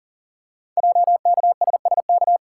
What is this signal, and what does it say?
Morse code, heard as a keyed tone, about 40 wpm, 700 Hz. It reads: JKSSK
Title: Code Groups - 5 Letter:Numbers - 40wpm